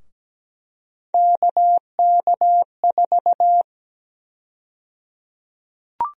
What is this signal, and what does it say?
Morse code, heard as a keyed tone, about 17 wpm, 700 Hz. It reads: KK4
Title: USA Call Sign Prefixes - 17wpm